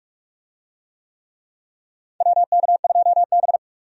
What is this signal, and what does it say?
Morse code, heard as a keyed tone, about 44 wpm, 700 Hz. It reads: WK2B
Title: USA Call Signs - 4 Characters - 45wpm